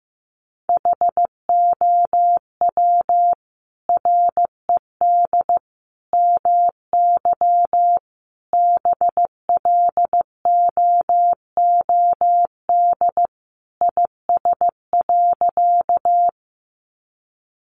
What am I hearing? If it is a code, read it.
Morse code, heard as a keyed tone, about 15 wpm, 700 Hz. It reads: HOW RED MY BLOOD IS.
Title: Sentences from Top 1100 Words 15wpm